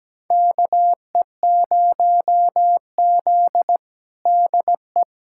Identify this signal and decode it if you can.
Morse code, heard as a keyed tone, about 17 wpm, 700 Hz. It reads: KE0Z DE
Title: USA Callsign DE Callsign 17wpm